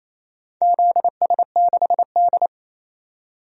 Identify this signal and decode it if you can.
Morse code, heard as a keyed tone, about 28 wpm, 700 Hz. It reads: ZS6D
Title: DXCC-100 Call Signs - 4 Characters - 28wpm